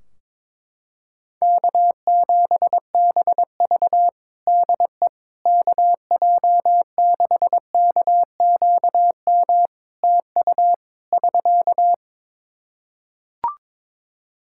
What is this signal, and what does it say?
Morse code, heard as a keyed tone, about 22 wpm, 700 Hz. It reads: K7BV DE KJ6KQM TU <SK>
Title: Practice QSO - Exchange 4 - Ending - 22wpm